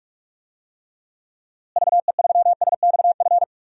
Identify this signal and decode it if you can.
Morse code, heard as a keyed tone, about 46 wpm, 700 Hz. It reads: VE3SXF
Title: DXCC-100 Call Signs - 6 Characters - Repeated 2x - 45wpm